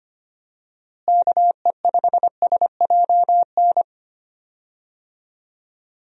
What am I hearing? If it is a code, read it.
Morse code, heard as a keyed tone, about 25 wpm, 700 Hz. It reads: KE5SJN
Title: USA Call Signs - 6 Characters - Repeated 3x - 25wpm